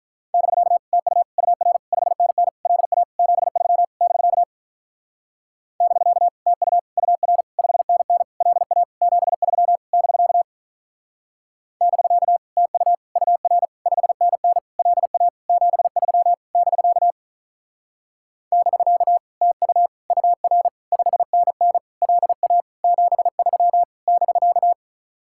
Morse code, heard as a keyed tone, about 45 wpm, 700 Hz. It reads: <BK> TU UR 5NN LA 73 <BK> <BK> TU UR 5NN LA 73 <BK> <BK> TU UR 5NN LA 73 <BK> <BK> TU UR 5NN LA 73 <BK>